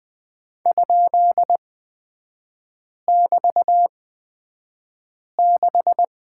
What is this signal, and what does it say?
Morse code, heard as a keyed tone, about 20 wpm, spaced as for 5 wpm, 700 Hz. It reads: ?<BT>6
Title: 255 Lesson 41e - Period - Three Characters 20wpm